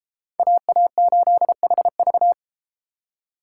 Morse code, heard as a keyed tone, about 32 wpm, 700 Hz. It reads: AA8HV